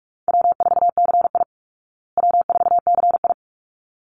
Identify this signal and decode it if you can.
Morse code, heard as a keyed tone, about 44 wpm, 700 Hz. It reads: W4CI W4CI
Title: Easy Call Signs Repeated 2x - 45wpm